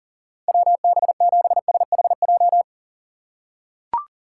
Morse code, heard as a keyed tone, about 40 wpm, 700 Hz. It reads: WB7SHJ
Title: USA Call Signs - 6 Characters - 4x Character Spacing - 40wpm